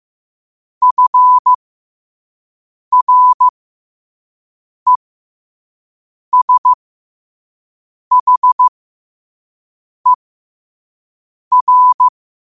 Morse code, heard as a keyed tone, about 15 wpm, spaced as for 5 wpm, 1000 Hz. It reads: FRESHER
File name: qotdfiles / 5wpm